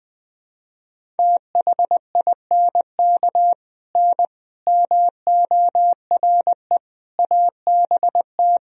Morse code, heard as a keyed tone, about 20 wpm, 700 Hz. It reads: THINK N MORE ABT
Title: Sentences from Top 200 Words and Abbreviations 20wpm